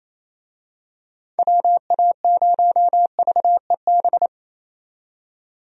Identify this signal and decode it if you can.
Morse code, heard as a keyed tone, about 28 wpm, 700 Hz. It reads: WA0VEB
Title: USA Call Signs - 6 Characters - 28wpm